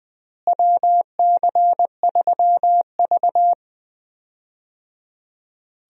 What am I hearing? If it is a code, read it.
Morse code, heard as a keyed tone, about 20 wpm, 690 Hz. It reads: WC3V
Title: USA Call Signs - 4 Characters - Repeated 3x - 20wpm